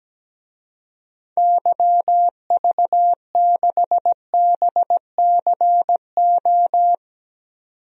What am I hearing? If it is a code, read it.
Morse code, heard as a keyed tone, about 17 wpm, 700 Hz. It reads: YV6BCO
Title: DXCC-100 Call Signs - 6 Characters - 17wpm